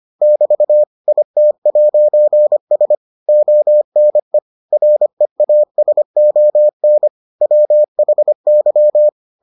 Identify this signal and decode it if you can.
Morse code, heard as a keyed tone, about 25 wpm, 600 Hz. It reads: <BT> IT'S ONE REASON WHY